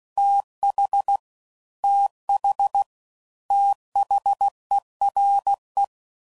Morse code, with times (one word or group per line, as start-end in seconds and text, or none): 0.2-1.2 TH
1.8-2.8 TH
3.5-5.8 THERE